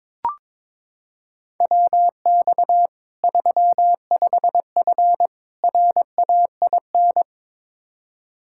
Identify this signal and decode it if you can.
Morse code, heard as a keyed tone, about 22 wpm, 700 Hz. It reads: WX 35F RAIN
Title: K1USN SST QSO Practice 22wpm